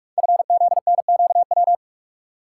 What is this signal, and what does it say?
Morse code, heard as a keyed tone, about 44 wpm, 690 Hz. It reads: FZNQW
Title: Code Groups - 5 Letters - 45wpm